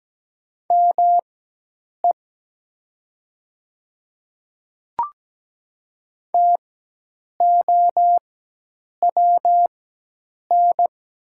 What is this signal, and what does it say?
Morse code, heard as a keyed tone, about 17 wpm, spaced as for 8 wpm, 700 Hz. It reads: ME TOWN
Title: 4x Character Spacing - Top 500 Words - 17wpm